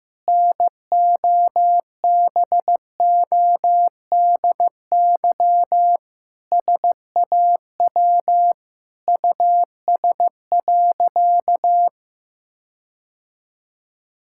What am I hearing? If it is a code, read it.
Morse code, heard as a keyed tone, about 16 wpm, 700 Hz. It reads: NOBODY SAW US.
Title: Sentences from Top 1600 Words 15wpm